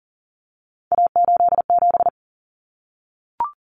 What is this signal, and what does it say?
Morse code, heard as a keyed tone, about 40 wpm, 700 Hz. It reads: A87
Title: Common Cut Numbers - 3-Digit 40wpm